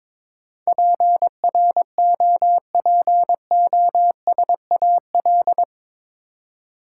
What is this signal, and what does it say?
Morse code, heard as a keyed tone, about 22 wpm, 700 Hz. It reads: PROPOSAL
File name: practice / 22wpm / Eight Letter Words - 22wpm